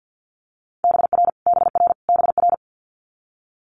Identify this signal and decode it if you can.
Morse code, heard as a keyed tone, about 50 wpm, 700 Hz. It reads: BR BR BR
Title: Multi-letter Phonemes Repeated 3x 50wpm